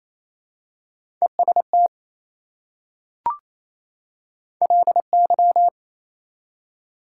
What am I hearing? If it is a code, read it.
Morse code, heard as a keyed tone, about 28 wpm, 700 Hz. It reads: EST LY
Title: Prefixes and Suffixes 28wpm